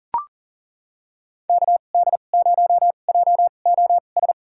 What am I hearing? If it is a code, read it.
Morse code, heard as a keyed tone, about 38 wpm, spaced as for 28 wpm, 700 Hz. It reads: KD0JOS
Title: USA Call Signs - 2x Character Spacing - 40wpm